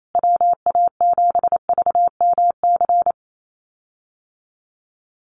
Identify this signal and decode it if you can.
Morse code, heard as a keyed tone, about 28 wpm, 700 Hz. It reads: WA7VMC
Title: USA Call Signs Repeated 3x 28wpm